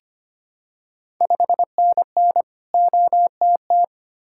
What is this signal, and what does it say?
Morse code, heard as a keyed tone, about 25 wpm, 700 Hz. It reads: 5NN OTT
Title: Ontario QSO Party Exchange 25wpm